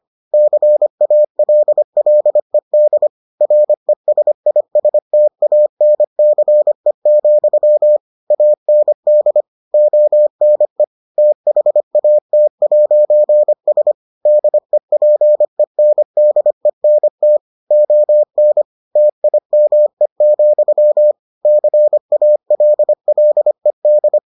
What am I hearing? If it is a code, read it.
Morse code, heard as a keyed tone, about 25 wpm, 600 Hz. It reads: CALLED RESISTANCE, AND ONE THAT'S DEPENDENT ON TIME, CALLED